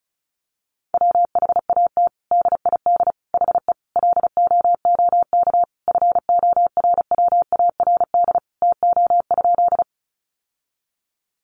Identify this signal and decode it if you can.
Morse code, heard as a keyed tone, about 34 wpm, 700 Hz. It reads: WHAT DID HE LOOK FORWARD TO?
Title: Sentences from Top 1300 Words 35wpm